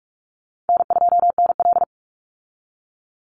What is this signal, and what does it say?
Morse code, heard as a keyed tone, about 46 wpm, 700 Hz. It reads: N2NL
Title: Easy Call Signs - 45wpm